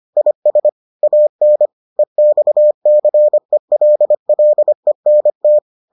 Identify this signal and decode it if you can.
Morse code, heard as a keyed tone, about 25 wpm, 600 Hz. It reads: IS AN EXCELLENT